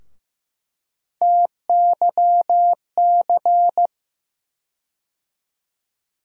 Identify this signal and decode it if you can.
Morse code, heard as a keyed tone, about 15 wpm, 700 Hz. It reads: TYC
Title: USA Call Sign Postfixes - 3 Letters 15wpm